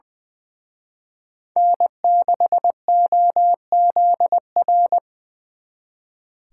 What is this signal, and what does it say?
Morse code, heard as a keyed tone, about 23 wpm, 700 Hz. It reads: N6OZR
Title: Speed Racing - USA 1x3 Call Signs 20wpm